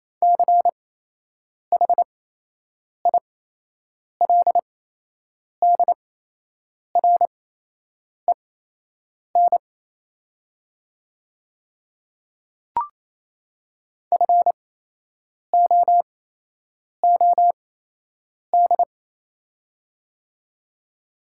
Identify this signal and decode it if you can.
Morse code, heard as a keyed tone, about 27 wpm, spaced as for 8 wpm, 700 Hz. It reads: CHILDREN FOOD
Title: 8x Character Spacing - Top 300 Words - 28wpm